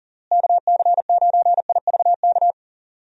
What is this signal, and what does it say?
Morse code, heard as a keyed tone, about 40 wpm, 700 Hz. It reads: KC9IVK
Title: USA Call Signs - 6 Characters - Repeated 3x - 40wpm